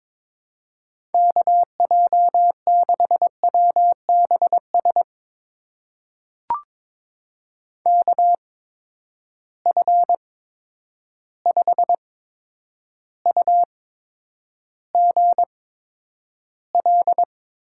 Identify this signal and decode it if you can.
Morse code, heard as a keyed tone, about 21 wpm, spaced as for 9 wpm, 700 Hz. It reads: KJ6WBS KF5UGL
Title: USA Call Signs - 6 Characters - 8x Character Spacing - 22wpm